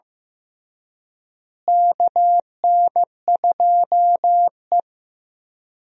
Morse code, heard as a keyed tone, about 15 wpm, 700 Hz. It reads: KN2E
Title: USA Call Signs - 4 Characters - Repeated 3x - 15wpm